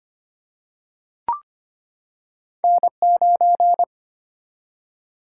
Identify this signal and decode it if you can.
Morse code, heard as a keyed tone, about 25 wpm, 700 Hz. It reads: N9
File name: Two Letter-Number Combinations - 25wpm